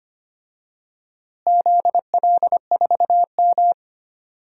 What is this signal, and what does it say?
Morse code, heard as a keyed tone, about 25 wpm, 700 Hz. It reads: ZL4M
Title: DXCC-100 Call Signs - 4 Characters - Repeated 2x - 25wpm